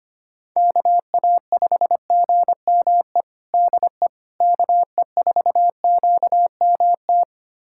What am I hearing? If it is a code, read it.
Morse code, heard as a keyed tone, about 25 wpm, 700 Hz. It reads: KA5GME DE KE4QMT
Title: USA Callsign DE Callsign - 6 Characters - 25wpm